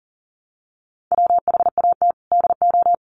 Morse code, heard as a keyed tone, about 40 wpm, 700 Hz. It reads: WHAT DO
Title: Sets of 2 Words 40wpm